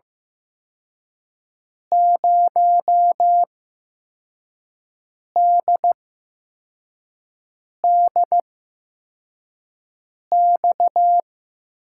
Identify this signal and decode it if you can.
Morse code, heard as a keyed tone, about 15 wpm, spaced as for 4 wpm, 700 Hz. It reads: 0DDX
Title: USA Call Signs - 6 Characters - 8x Character Spacing - 15wpm